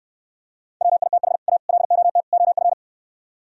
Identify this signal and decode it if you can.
Morse code, heard as a keyed tone, about 68 wpm, 700 Hz. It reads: WITH A LOT OF